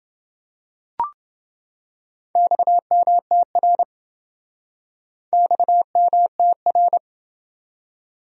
Morse code, heard as a keyed tone, about 27 wpm, 700 Hz. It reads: XMTR XMTR